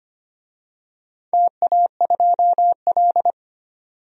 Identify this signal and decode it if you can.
Morse code, heard as a keyed tone, about 25 wpm, 700 Hz. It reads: TA2L
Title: DXCC-100 Call Signs - 4 Characters - 25wpm